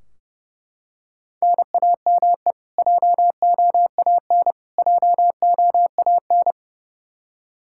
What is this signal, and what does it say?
Morse code, heard as a keyed tone, about 30 wpm, 700 Hz. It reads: NAME JOAN JOAN
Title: Practice QSO - Exchange 1 - Beginning - 30wpm